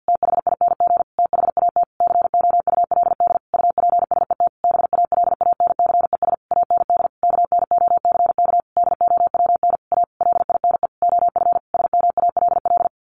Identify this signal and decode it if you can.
Morse code, heard as a keyed tone, about 50 wpm, 700 Hz. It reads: THING THAT COULD UPSET BALANCES AND KNOCK DOWN A LINE OF SMALL
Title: A Sound Of Thunder 50wpm